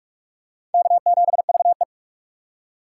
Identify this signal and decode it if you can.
Morse code, heard as a keyed tone, about 42 wpm, 700 Hz. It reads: K7VE
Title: USA Call Signs - 4x Character Spacing - 45wpm